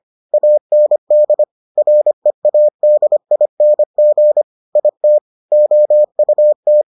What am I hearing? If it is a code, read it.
Morse code, heard as a keyed tone, about 25 wpm, 600 Hz. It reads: AND READING IT OUT